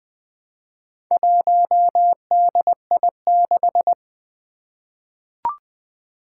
Morse code, heard as a keyed tone, about 20 wpm, 700 Hz. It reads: 1DI6
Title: Four Challenging Letter-Number Combinations 20wpm